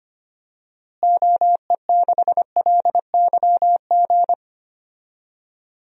Morse code, heard as a keyed tone, about 25 wpm, 700 Hz. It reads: OE6LYG
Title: DXCC-100 Call Signs - 6 Characters - Repeated 2x - 25wpm